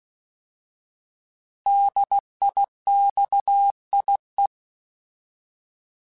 Morse code, heard as a keyed tone, about 16 wpm, 800 Hz. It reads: DIXIE